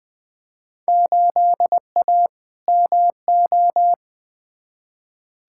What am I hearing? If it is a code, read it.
Morse code, heard as a keyed tone, about 20 wpm, 700 Hz. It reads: 8A MO
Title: Field Day Exchanges Encore - 20wpm